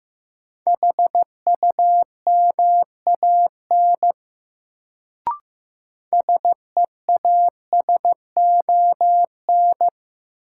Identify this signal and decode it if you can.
Morse code, heard as a keyed tone, about 15 wpm, 700 Hz. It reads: HUMAN SEASON
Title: Sets of 5 Words - New Word Drill - 15wpm